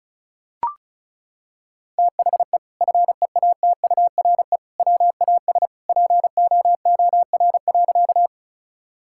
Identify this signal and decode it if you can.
Morse code, heard as a keyed tone, about 34 wpm, 700 Hz. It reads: THE FEATURE WAS POOR.
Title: Sentences from Top 1000 Words - Review Part 5 - 35wpm